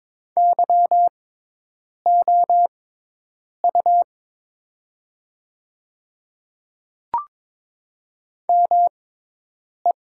Morse code, heard as a keyed tone, about 22 wpm, spaced as for 8 wpm, 700 Hz. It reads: YOU ME
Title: 6x Character Spacing - Top 200 Words - 22wpm